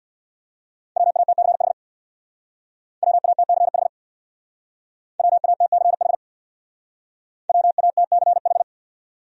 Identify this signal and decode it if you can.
Morse code, heard as a keyed tone, about 62 wpm, 700 Hz. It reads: WATCH WATCH WATCH WATCH
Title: Speed-Racing Top 500 Words 50wpm